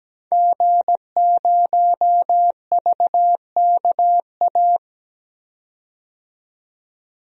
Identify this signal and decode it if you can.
Morse code, heard as a keyed tone, about 17 wpm, 700 Hz. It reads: G0VKA
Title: DXCC-100 Call Signs - 5 Characters - 17wpm